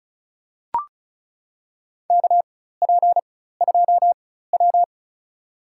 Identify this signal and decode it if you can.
Morse code, heard as a keyed tone, about 34 wpm, spaced as for 16 wpm, 690 Hz. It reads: KP2W
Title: USA Call Signs - 4x Character Spacing - 35wpm